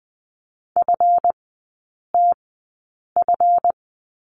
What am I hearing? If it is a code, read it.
Morse code, heard as a keyed tone, about 20 wpm, spaced as for 8 wpm, 690 Hz. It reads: FTF
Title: 191 Lesson 31e - Letter v - Three Characters 20wpm